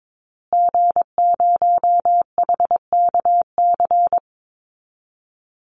Morse code, heard as a keyed tone, about 22 wpm, 700 Hz. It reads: G0HKC
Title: Call Signs Repeated 2x - 22wpm